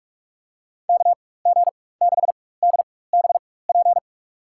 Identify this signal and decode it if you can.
Morse code, heard as a keyed tone, about 46 wpm, spaced as for 21 wpm, 700 Hz. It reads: KG6DBP
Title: USA Call Signs - 6 Characters - 4x Character Spacing - 45wpm